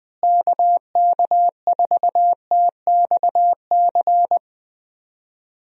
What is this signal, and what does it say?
Morse code, heard as a keyed tone, about 20 wpm, 700 Hz. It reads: KK4TXC
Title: USA Call Signs - 6 Characters - Repeated 3x - 20wpm